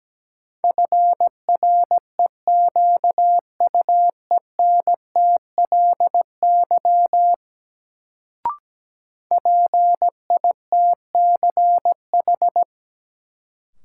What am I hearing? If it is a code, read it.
Morse code, heard as a keyed tone, about 17 wpm, 700 Hz. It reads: FREQUENTLY PITCH
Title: Sentences from Top 1300 Words - New Word Drill Alt - 17wpm